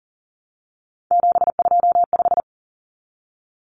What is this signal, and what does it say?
Morse code, heard as a keyed tone, about 40 wpm, 700 Hz. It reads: 725
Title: Common Cut Numbers - 3-Digit 40wpm